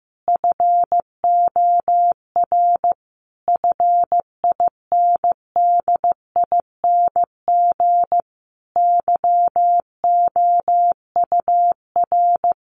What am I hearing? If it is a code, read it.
Morse code, heard as a keyed tone, about 15 wpm, 700 Hz. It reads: FOR FINDING YOUR
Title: Sentences from Top 1300 Words 15wpm